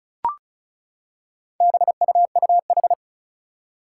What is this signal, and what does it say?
Morse code, heard as a keyed tone, about 34 wpm, 700 Hz. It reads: BUUH
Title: Four Challenging Letter-Number Combinations 35wpm